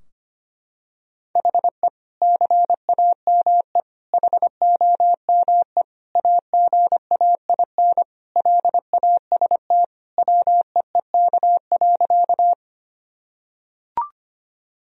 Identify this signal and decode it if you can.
Morse code, heard as a keyed tone, about 25 wpm, 700 Hz. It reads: HE CAME HOME AGAIN LAST WEEK.
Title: Sentences from Top 1000 Words - Review Part 2 - 25wpm